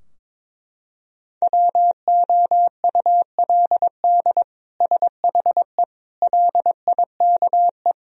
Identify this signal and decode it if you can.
Morse code, heard as a keyed tone, about 22 wpm, 700 Hz. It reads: WOULD SHE LIKE
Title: Sentences from Top 1000 Words - Review Part 2 - 22wpm